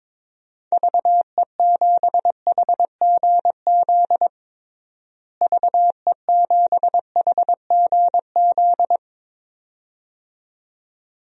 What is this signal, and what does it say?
Morse code, heard as a keyed tone, about 22 wpm, 700 Hz. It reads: VE7HGZ VE7HGZ
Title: DXCC-100 Call Signs - 6 Characters - Repeated 2x - 22wpm